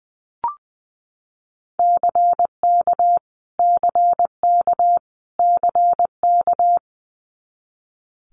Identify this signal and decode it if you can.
Morse code, heard as a keyed tone, about 20 wpm, 700 Hz. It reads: CK CK CK
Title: Multi-letter Phonemes Repeated 3x 20wpm